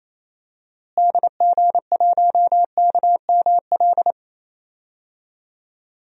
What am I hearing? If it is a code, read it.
Morse code, heard as a keyed tone, about 28 wpm, 700 Hz. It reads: DG1KML
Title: DXCC-100 Call Signs - 6 Characters - Repeated 3x - 28wpm